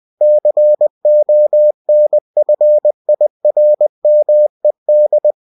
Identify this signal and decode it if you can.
Morse code, heard as a keyed tone, about 20 wpm, 600 Hz. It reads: CONFIRMED
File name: cw-books / winnie-the-pooh / winnie-20wpm-page-0122